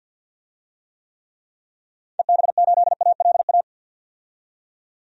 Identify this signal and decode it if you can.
Morse code, heard as a keyed tone, about 45 wpm, 700 Hz. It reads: EB8ALA
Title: DXCC-100 Call Signs - 6 Characters - 50wpm